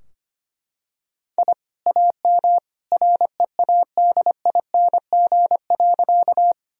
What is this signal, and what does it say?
Morse code, heard as a keyed tone, about 25 wpm, 700 Hz. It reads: I AM READING.
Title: Sentences from Top 1000 Words - Review Part 3 - 25wpm